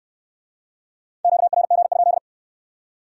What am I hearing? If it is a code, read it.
Morse code, heard as a keyed tone, about 56 wpm, 700 Hz. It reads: QRG?